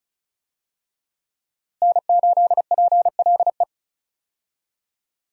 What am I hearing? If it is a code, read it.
Morse code, heard as a keyed tone, about 36 wpm, 700 Hz. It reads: N8PLE
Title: USA Call Signs - 5 Characters - Repeated 3x - 35wpm